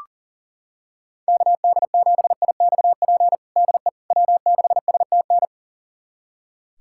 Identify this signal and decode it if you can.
Morse code, heard as a keyed tone, about 37 wpm, 700 Hz. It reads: KD7IXP DE W6STN
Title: USA Callsign DE Callsign 40wpm